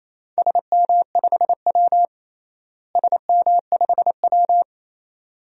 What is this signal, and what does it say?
Morse code, heard as a keyed tone, about 28 wpm, 700 Hz. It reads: SM5W SM5W
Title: DXCC-100 Call Signs - 4 Characters - Repeated 2x - 28wpm